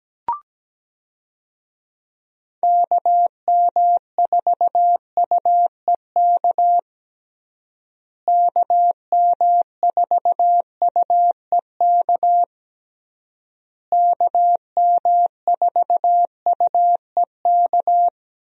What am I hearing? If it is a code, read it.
Morse code, heard as a keyed tone, about 17 wpm, 700 Hz. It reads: KM4UEK KM4UEK KM4UEK
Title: USA Call Signs - 6 Characters - Repeated 3x - 17wpm